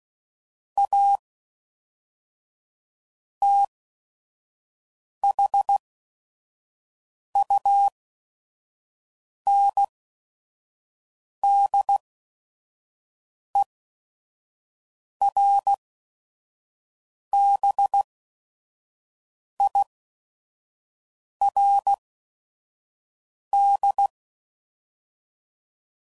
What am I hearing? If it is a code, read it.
Morse code, heard as a keyed tone, about 16 wpm, spaced as for 5 wpm, 800 Hz. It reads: A THUNDERBIRD
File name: CW Code Groups @5WPM / QSOs / WD0FBT 1